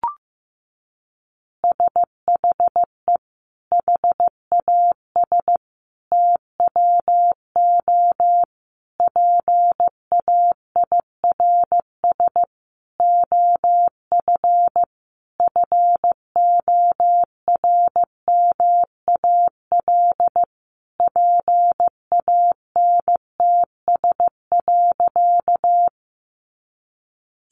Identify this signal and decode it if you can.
Morse code, heard as a keyed tone, about 14 wpm, 700 Hz. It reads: SHE HAS TWO PAIRS OF FORMAL PANTS.
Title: Additional Easy Phrases 15wpm